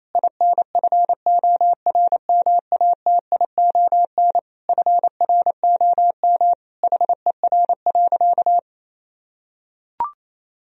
Morse code, heard as a keyed tone, about 27 wpm, 700 Hz. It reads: INFORMATION FROM HER.
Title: Sentences from Top 1000 Words - Review Part 3 - 28wpm